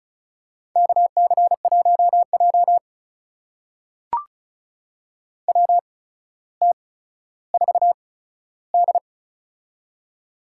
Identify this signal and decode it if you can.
Morse code, heard as a keyed tone, about 36 wpm, spaced as for 15 wpm, 700 Hz. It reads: KC1J WT4D
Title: USA Call Signs - 4 Characters - 8x Character Spacing - 35wpm